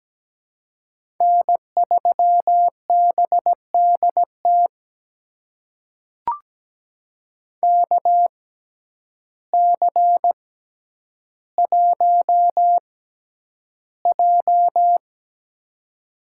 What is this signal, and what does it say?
Morse code, heard as a keyed tone, about 17 wpm, spaced as for 9 wpm, 700 Hz. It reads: N3BDT KC1J
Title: USA Call Signs - 6x Character Spacing - 17wpm